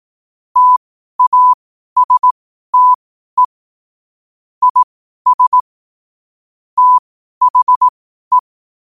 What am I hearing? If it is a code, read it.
Morse code, heard as a keyed tone, about 18 wpm, spaced as for 12 wpm, 1000 Hz. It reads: TASTE IS THE
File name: qotdfiles / 12wpm